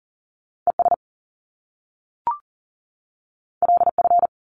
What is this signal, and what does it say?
Morse code, heard as a keyed tone, about 40 wpm, 700 Hz. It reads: ES LF
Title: Top Two Letter Combinations in Words 40wpm